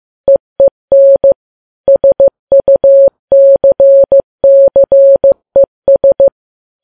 Morse code, heard as a keyed tone, about 15 wpm, 550 Hz. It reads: EEN SUCCES
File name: Files Web tls / Morse on9fv / morse15kps